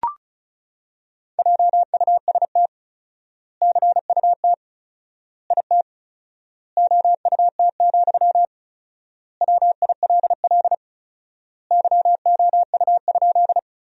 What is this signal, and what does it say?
Morse code, heard as a keyed tone, about 34 wpm, 700 Hz. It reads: JUST CUT IT OUT, WILL YOU?
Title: 4x Word-Spacing - Sentences from Top 300 Words - 35wpm